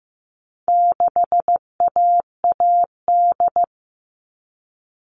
Morse code, heard as a keyed tone, about 15 wpm, 700 Hz. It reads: 6AAD
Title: All Cut Numbers - 4-Digit 15wpm